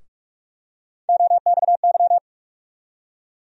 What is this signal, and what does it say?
Morse code, heard as a keyed tone, about 47 wpm, 700 Hz. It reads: OXY
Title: USA Call Sign Postfixes - 3 Letters 45wpm